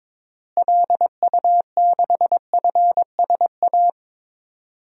Morse code, heard as a keyed tone, about 22 wpm, 700 Hz. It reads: LU6FSA
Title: DXCC-100 Call Signs - 6 Characters - Repeated 3x - 22wpm